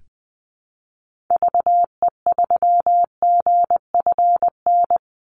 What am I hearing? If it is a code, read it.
Morse code, heard as a keyed tone, about 20 wpm, 700 Hz. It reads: VE3GFN
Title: Call Signs - 20wpm